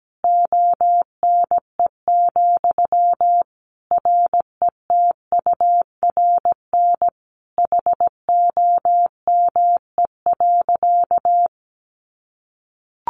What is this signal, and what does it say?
Morse code, heard as a keyed tone, about 17 wpm, 700 Hz. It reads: ONE, RETURN HOME.
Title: Sentences from Top 800 Words 17wpm